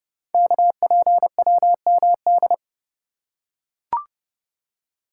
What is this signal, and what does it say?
Morse code, heard as a keyed tone, about 30 wpm, 700 Hz. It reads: KPWMD
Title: Code Groups - 5 Letters - 30wpm